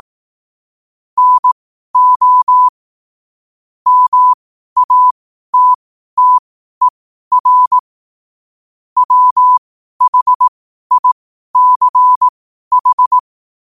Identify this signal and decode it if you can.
Morse code, heard as a keyed tone, about 18 wpm, spaced as for 12 wpm, 1000 Hz. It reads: NO MATTER WHICH
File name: qotdfiles / 12wpm